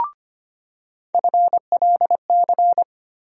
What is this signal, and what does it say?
Morse code, heard as a keyed tone, about 25 wpm, 700 Hz. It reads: FLC
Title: USA Call Sign Postfixes - 3 Letters 25wpm